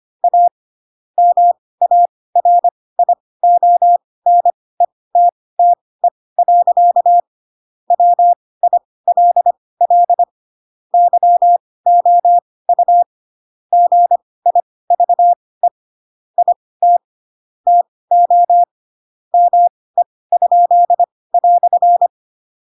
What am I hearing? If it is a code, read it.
Morse code, heard as a keyed tone, about 25 wpm, spaced as for 18 wpm, 700 Hz. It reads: A MARIONETTE. WILL YOU GIVE IT TO ME?"
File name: Pinocchio / Pinocchio ch2 12 25 ik1hge